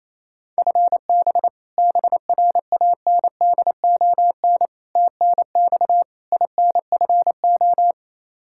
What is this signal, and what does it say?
Morse code, heard as a keyed tone, about 28 wpm, 700 Hz. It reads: FB BRANDON TNX INFO
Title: Practice QSO - Exchange 3 - More Info - 28wpm